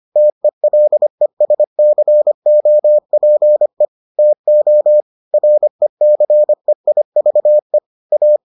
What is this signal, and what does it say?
Morse code, heard as a keyed tone, about 25 wpm, 600 Hz. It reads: TELESCOPE TO RECEIVE A